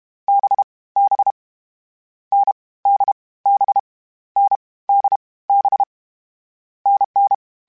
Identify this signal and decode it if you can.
Morse code, heard as a keyed tone, about 32 wpm, 800 Hz. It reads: B B N D B N D B NN
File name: Difficult Combos / Difficult Combos-mp3s / NDB NDB6 @30